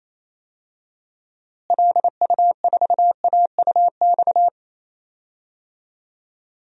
Morse code, heard as a keyed tone, about 28 wpm, 700 Hz. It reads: LU4AUX
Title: DXCC-100 Call Signs - 6 Characters - Repeated 2x - 28wpm